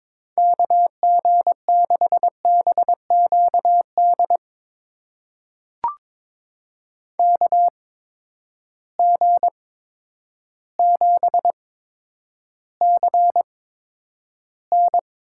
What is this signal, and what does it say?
Morse code, heard as a keyed tone, about 22 wpm, spaced as for 10 wpm, 700 Hz. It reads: KG6BQD KG7CN
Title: USA Call Signs - 6 Characters - 8x Character Spacing - 22wpm